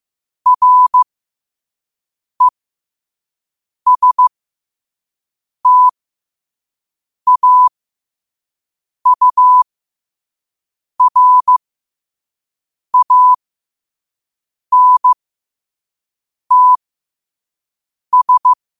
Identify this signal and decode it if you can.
Morse code, heard as a keyed tone, about 15 wpm, spaced as for 5 wpm, 1000 Hz. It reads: RESTAURANTS
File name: qotdfiles / 5wpm